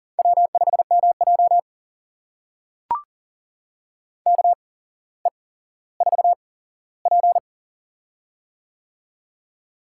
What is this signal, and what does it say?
Morse code, heard as a keyed tone, about 39 wpm, spaced as for 17 wpm, 700 Hz. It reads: W5MJ KE4P
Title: USA Call Signs - 4 Characters - 8x Character Spacing - 40wpm